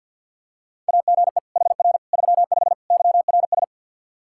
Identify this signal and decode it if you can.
Morse code, heard as a keyed tone, about 51 wpm, 700 Hz. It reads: AGE HR 35 YRS
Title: Practice QSO - Exchange 3 - More Info - 50wpm